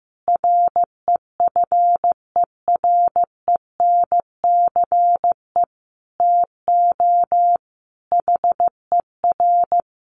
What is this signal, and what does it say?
Morse code, heard as a keyed tone, about 15 wpm, 700 Hz. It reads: REFERENCE TO HER
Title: Sentences from Top 1200 Words 15wpm